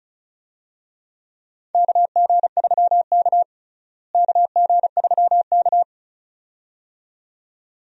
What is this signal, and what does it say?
Morse code, heard as a keyed tone, about 36 wpm, 700 Hz. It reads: KG3K KG3K
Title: USA Call Signs - 4 Characters - Repeated 2x - 35wpm